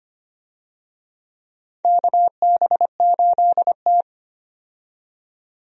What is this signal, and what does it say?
Morse code, heard as a keyed tone, about 25 wpm, 700 Hz. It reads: KB8T
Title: USA Call Signs - 4 Characters - 25wpm